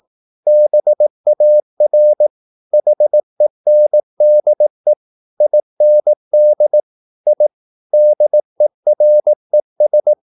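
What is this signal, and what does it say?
Morse code, heard as a keyed tone, about 18 wpm, 600 Hz. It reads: BAR HENDE IND I DERES
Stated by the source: Slow (QRS) Morse practice at 18 wpm, CW med Gnister.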